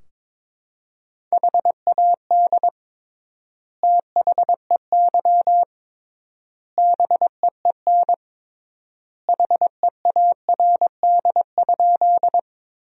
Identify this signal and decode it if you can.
Morse code, heard as a keyed tone, about 22 wpm, 700 Hz. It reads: HAD THEY BEEN HEARD?
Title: 3x Word-Spacing - Sentences from Top 400 Words - 22wpm